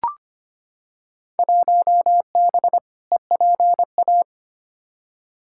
Morse code, heard as a keyed tone, about 25 wpm, 700 Hz. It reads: 1B EPA
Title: Field Day Exchanges - 25wpm